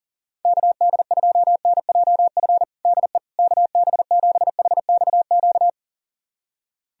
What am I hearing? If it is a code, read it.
Morse code, heard as a keyed tone, about 37 wpm, 700 Hz. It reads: KD2NJF DE KB7HXQ
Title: USA Callsign DE Callsign - 6 Characters - 40wpm